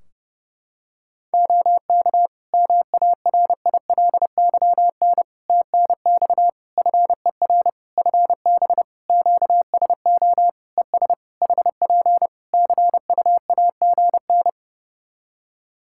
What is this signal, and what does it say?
Morse code, heard as a keyed tone, about 28 wpm, 700 Hz. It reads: OK MARILYN TNX FER FB QSO ES HP CUAGN
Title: Practice QSO - Exchange 4 - Ending - 30wpm